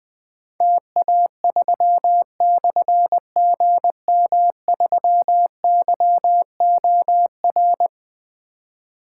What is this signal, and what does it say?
Morse code, heard as a keyed tone, about 20 wpm, 700 Hz. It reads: TA3/GM3YOR
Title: Difficult Call Signs 20wpm